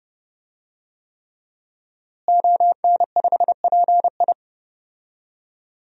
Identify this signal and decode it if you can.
Morse code, heard as a keyed tone, about 30 wpm, 700 Hz. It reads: ON5PI
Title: DXCC-100 Call Signs - 5 Characters - Repeated 3x - 30wpm